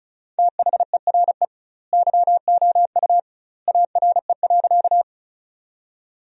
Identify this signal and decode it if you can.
Morse code, heard as a keyed tone, about 36 wpm, 700 Hz. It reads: THERE YOU ARE.
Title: 2x Word-Spacing - Sentences from Top 100 Words - 35wpm